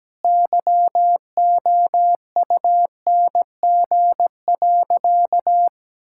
Morse code, heard as a keyed tone, about 17 wpm, 700 Hz. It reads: YOUNG.
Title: Sentences from Top 1000 Words - Review Part 6 - 17wpm